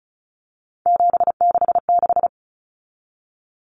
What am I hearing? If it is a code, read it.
Morse code, heard as a keyed tone, about 34 wpm, 700 Hz. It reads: 766
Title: Numbers - 3-Digit 35wpm